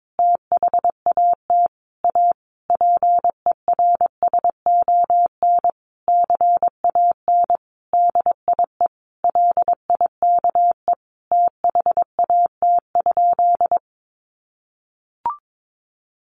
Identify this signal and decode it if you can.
Morse code, heard as a keyed tone, about 22 wpm, 700 Hz. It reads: THAT A PERSON CAN DIE LIKE THAT?
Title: Sentences from Top 1400 Words 22wpm